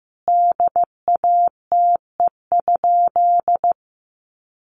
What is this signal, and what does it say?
Morse code, heard as a keyed tone, about 15 wpm, 700 Hz. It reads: DATE?
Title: Sentences from Top 1400 Words 15wpm